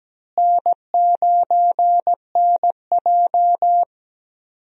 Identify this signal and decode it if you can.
Morse code, heard as a keyed tone, about 17 wpm, 700 Hz. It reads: N9NJ
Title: USA Call Signs - 4 Characters - Repeated 2x - 17wpm